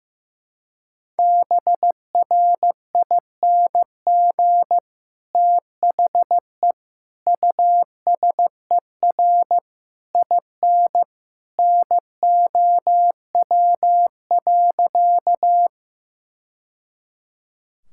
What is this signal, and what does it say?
Morse code, heard as a keyed tone, about 15 wpm, 700 Hz. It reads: BRING THE USER IN NOW.
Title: Sentences from Top 1000 Words - Review Part 5 - 15wpm